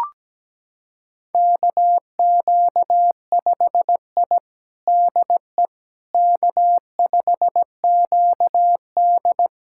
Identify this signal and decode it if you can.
Morse code, heard as a keyed tone, about 17 wpm, 700 Hz. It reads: KQ5I DE K5QD
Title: USA Callsign DE Callsign - 4 Characters - 17wpm